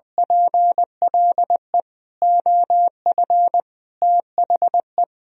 Morse code, heard as a keyed tone, about 21 wpm, 700 Hz. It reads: PLE OF THE
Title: Sentences from Top 1000 Words - Review Part 1 - 20wpm